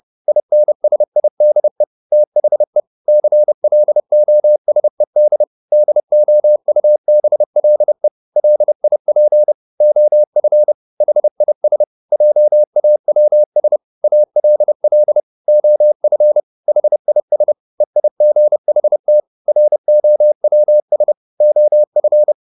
Morse code, heard as a keyed tone, about 30 wpm, 600 Hz. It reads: INSIDE THE CLOSED DOUBLE LIP OF HIS JAWS ALL OF HIS EIGHT ROWS OF